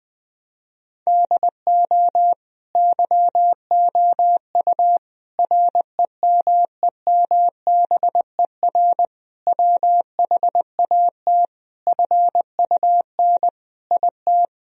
Morse code, heard as a keyed tone, about 20 wpm, 700 Hz. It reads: DO YOU REMEMBER WHAT FUN IT
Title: Sentences from Top 1000 Words - Review Part 4 - 20wpm